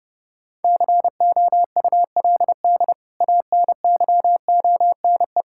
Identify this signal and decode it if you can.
Morse code, heard as a keyed tone, about 30 wpm, 700 Hz. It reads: COULD ANYONE
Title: Sentences from Top 1600 Words 30wpm